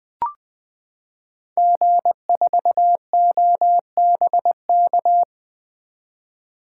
Morse code, heard as a keyed tone, about 20 wpm, 700 Hz. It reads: G4OBK
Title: 2023 SOTA Activator Call Signs - 20wpm